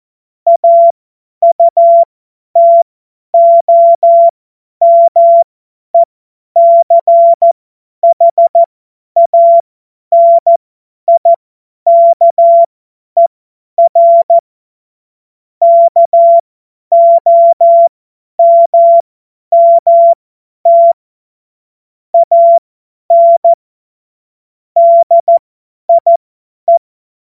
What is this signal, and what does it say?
Morse code, heard as a keyed tone, about 14 wpm, spaced as for 10 wpm, 680 Hz. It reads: AUTOMECHANIKER KOMMT AN DIE
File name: Morsekurs / Witze / Witz 13. Januar 2026 50 F70